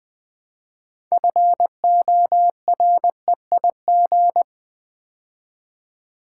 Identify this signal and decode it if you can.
Morse code, heard as a keyed tone, about 20 wpm, 700 Hz. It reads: FOREIG
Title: Build A Word - 3 Letters At A Time - 9-Letter Words - 20wpm